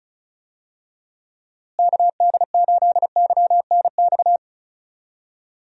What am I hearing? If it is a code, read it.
Morse code, heard as a keyed tone, about 34 wpm, 700 Hz. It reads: KD8YNX
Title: USA Call Signs - 6 Characters - Repeated 2x - 35wpm